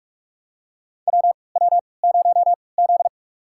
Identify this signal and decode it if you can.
Morse code, heard as a keyed tone, about 46 wpm, spaced as for 26 wpm, 700 Hz. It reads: WW0Z
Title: USA Call Signs - 4 Characters - 3x Character Spacing - 45wpm